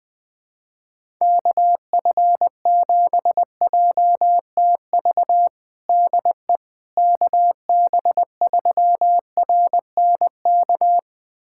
Morse code, heard as a keyed tone, about 19 wpm, 700 Hz. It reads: KF7JTV DE KB3RNK
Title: USA Callsign DE Callsign 20wpm